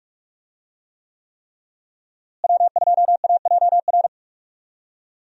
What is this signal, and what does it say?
Morse code, heard as a keyed tone, about 43 wpm, 700 Hz. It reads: W2AJR
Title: USA Call Signs - 5 Characters - Repeated 3x - 45wpm